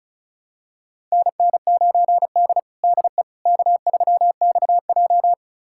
Morse code, heard as a keyed tone, about 36 wpm, 700 Hz. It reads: NN9D DE K3XJ
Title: USA Callsign DE Callsign - 4 Characters - 35wpm